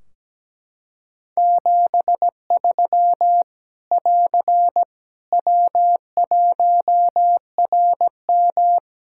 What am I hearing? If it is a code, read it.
Morse code, heard as a keyed tone, about 17 wpm, 700 Hz. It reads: 73 <AR> W1RM
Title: Practice QSO - Exchange 4 - Ending - 17wpm